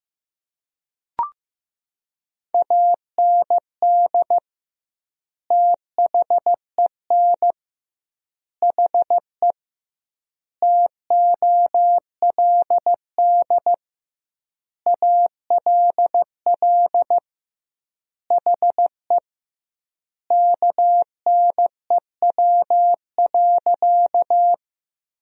Morse code, heard as a keyed tone, about 15 wpm, 700 Hz. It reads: AND THEN HE TOLD ALL HE KNEW.
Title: 2x Word-Spacing - Sentences from Top 400 Words - 15wpm